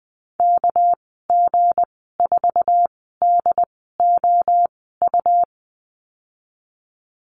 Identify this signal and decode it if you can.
Morse code, heard as a keyed tone, about 19 wpm, spaced as for 14 wpm, 700 Hz. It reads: KG4DOU
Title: 213 Lesson 34g - Number 8 - Call Signs 20wpm